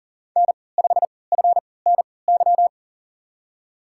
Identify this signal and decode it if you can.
Morse code, heard as a keyed tone, about 40 wpm, spaced as for 23 wpm, 700 Hz. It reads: N5FNY
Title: USA Call Signs - 3x Character Spacing - 40wpm